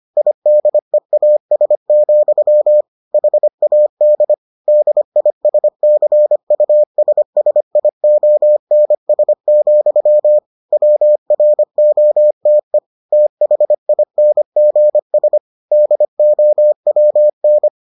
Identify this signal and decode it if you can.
Morse code, heard as a keyed tone, about 25 wpm, 600 Hz. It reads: IDEAS, HAD DISCUSSIONS, WROTE THINGS DOWN